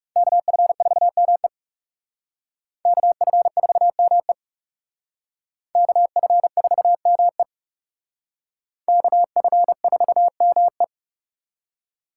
Morse code, heard as a keyed tone, about 38 wpm, 700 Hz. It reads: KF4ME KF4ME KF4ME KF4ME